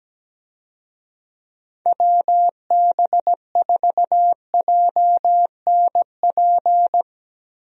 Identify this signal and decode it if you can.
Morse code, heard as a keyed tone, about 17 wpm, 700 Hz. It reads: WB4JNP
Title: USA Call Signs - 6 Characters - Repeated 3x - 17wpm